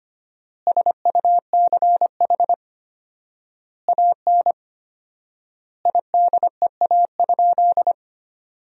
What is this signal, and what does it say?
Morse code, heard as a keyed tone, about 25 wpm, 700 Hz. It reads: SUCH AN IDEA?
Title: 4x Word-Spacing - Sentences from Top 400 Words - 25wpm